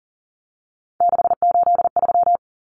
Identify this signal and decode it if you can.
Morse code, heard as a keyed tone, about 40 wpm, 700 Hz. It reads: 683
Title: Numbers - 3-Digit 40wpm